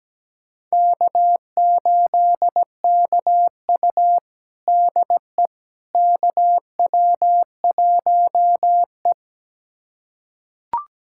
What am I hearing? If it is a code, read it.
Morse code, heard as a keyed tone, about 17 wpm, 700 Hz. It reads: K8KU DE KW1E
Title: USA Callsign DE Callsign - 4 Characters - 17wpm